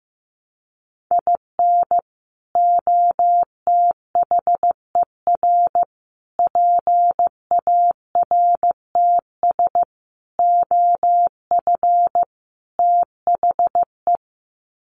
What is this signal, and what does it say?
Morse code, heard as a keyed tone, about 15 wpm, 700 Hz. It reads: IN OTHER PARTS OF THE
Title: Sets of 5 Words 15wpm